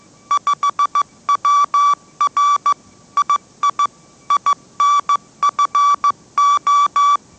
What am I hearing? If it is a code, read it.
Morse code, heard as a keyed tone, about 15 wpm, 1150 Hz. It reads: 5WR II INFO
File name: mrx / mrx065 16wpm